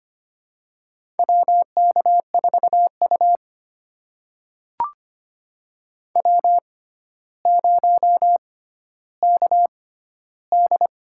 Morse code, heard as a keyed tone, about 25 wpm, spaced as for 13 wpm, 700 Hz. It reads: WK4U W0KD
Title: USA Call Signs - 4 Characters - 6x Character Spacing - 25wpm